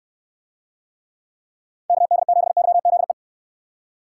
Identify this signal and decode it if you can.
Morse code, heard as a keyed tone, about 68 wpm, 700 Hz. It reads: KD7YZE